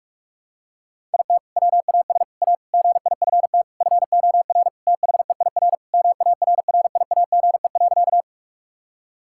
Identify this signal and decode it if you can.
Morse code, heard as a keyed tone, about 41 wpm, 700 Hz. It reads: IT WAS A GIFT FOR THEIR MARRIAGE.
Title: Sentences from Top 1000 Words - Review Part 6 - 45wpm